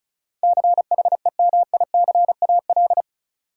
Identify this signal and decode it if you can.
Morse code, heard as a keyed tone, about 34 wpm, 700 Hz. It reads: CHEMICAL
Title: Sentences from Top 1200 Words - New Word Drill - 35wpm